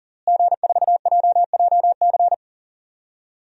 Morse code, heard as a keyed tone, about 40 wpm, 700 Hz. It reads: G4JJC
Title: DXCC-100 Call Signs - 5 Characters - Repeated 2x - 40wpm